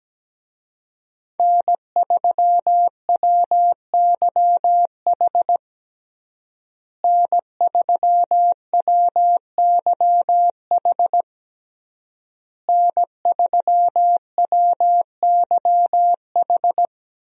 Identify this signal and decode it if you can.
Morse code, heard as a keyed tone, about 17 wpm, 700 Hz. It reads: N3WYH N3WYH N3WYH
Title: USA Call Signs - 5 Characters - Repeated 3x - 17wpm